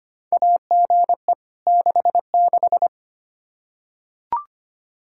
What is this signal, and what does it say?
Morse code, heard as a keyed tone, about 25 wpm, 700 Hz. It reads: AGE 66
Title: K1USN SST QSO Practice 25wpm